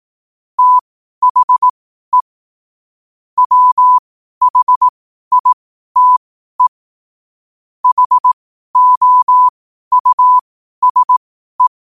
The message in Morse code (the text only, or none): THE WHITE HOUSE